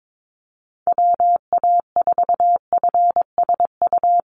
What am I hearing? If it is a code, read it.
Morse code, heard as a keyed tone, about 22 wpm, 700 Hz. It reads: WA4FSU
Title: USA Call Signs Repeated 2x 22wpm